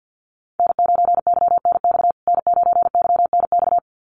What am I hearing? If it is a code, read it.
Morse code, heard as a keyed tone, about 50 wpm, 700 Hz. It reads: N9YNX N9YNX
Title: USA Call Signs Repeated 2x 50wpm